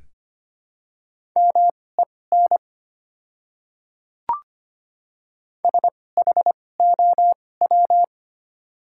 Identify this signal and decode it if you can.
Morse code, heard as a keyed tone, about 25 wpm, spaced as for 18 wpm, 700 Hz. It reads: MEN SHOW
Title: 2x Character Spacing - Top 300 Words - 25wpm